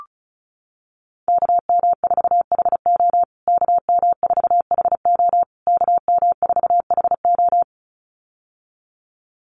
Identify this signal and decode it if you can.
Morse code, heard as a keyed tone, about 34 wpm, 700 Hz. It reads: KM4HO KM4HO KM4HO
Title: USA Call Signs Repeated 3x 35wpm